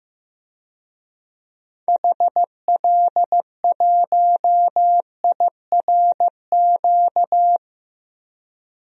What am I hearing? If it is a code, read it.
Morse code, heard as a keyed tone, about 15 wpm, 700 Hz. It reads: HL1IRQ
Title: DXCC-100 Call Signs - 6 Characters - 15wpm